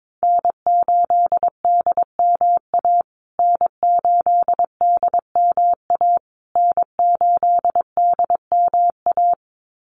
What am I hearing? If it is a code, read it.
Morse code, heard as a keyed tone, about 22 wpm, 700 Hz. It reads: N8DMA N8DMA N8DMA
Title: USA Call Signs Repeated 3x 22wpm